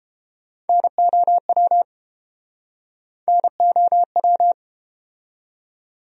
Morse code, heard as a keyed tone, about 30 wpm, 700 Hz. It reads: NOW NOW